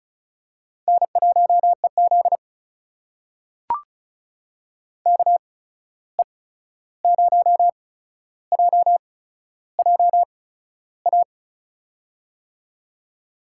Morse code, heard as a keyed tone, about 36 wpm, spaced as for 13 wpm, 700 Hz. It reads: N1EZ KE0JJA
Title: USA Call Signs - 8x Character Spacing - 35wpm